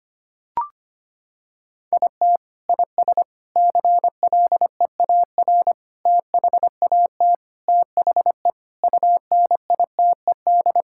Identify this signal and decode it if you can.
Morse code, heard as a keyed tone, about 25 wpm, 700 Hz. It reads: IT IS CLEAR THAT THE UNITED
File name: practice / 25wpm / Sentences with 4-Word Phrases - 25wpm